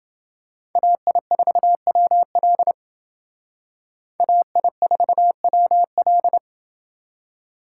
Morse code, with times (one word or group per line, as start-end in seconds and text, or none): 0.8-2.7 AI4WL
4.2-6.4 AI4WL